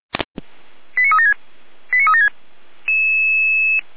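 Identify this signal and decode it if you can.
Five-tone selective call (ZVEI), heard as quick runs of five tones, tones of about 70 ms. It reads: 89376 89376
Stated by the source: ZVEI five-tone selective call received with an SDR.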